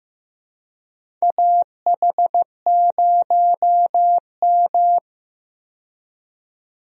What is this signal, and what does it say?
Morse code, heard as a keyed tone, about 15 wpm, 700 Hz. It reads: AH0M
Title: USA Call Signs - 4 Characters - 15wpm